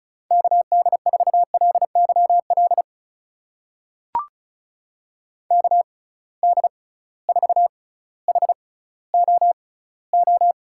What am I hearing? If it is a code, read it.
Morse code, heard as a keyed tone, about 36 wpm, spaced as for 18 wpm, 700 Hz. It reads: KD4LYL KD4HOO
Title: USA Call Signs - 6 Characters - 6x Character Spacing - 35wpm